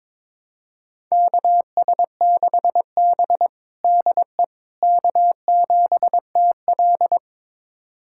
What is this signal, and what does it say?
Morse code, heard as a keyed tone, about 22 wpm, 700 Hz. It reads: KS6B DE K7TL
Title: USA Callsign DE Callsign - 4 Characters - 22wpm